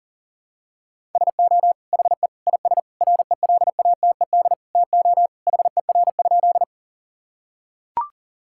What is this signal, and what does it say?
Morse code, heard as a keyed tone, about 37 wpm, 700 Hz. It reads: SO HE IS RELATED TO HER?
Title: Sentences from Top 1000 Words - Review Part 5 - 40wpm